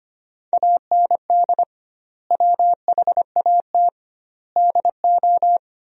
Morse code, heard as a keyed tone, about 25 wpm, 700 Hz. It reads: AND WHAT DO
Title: 2x Word-Spacing - Sentences from Top 100 Words - 25wpm